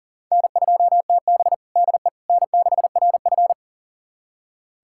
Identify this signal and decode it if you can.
Morse code, heard as a keyed tone, about 40 wpm, 700 Hz. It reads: N2TB DE N6RF
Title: USA Callsign DE Callsign - 4 Characters - 40wpm